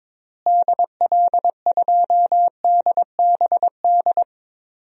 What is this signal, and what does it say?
Morse code, heard as a keyed tone, about 22 wpm, 700 Hz. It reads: DL2DBD
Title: DXCC-100 Call Signs - 6 Characters - Repeated 2x - 22wpm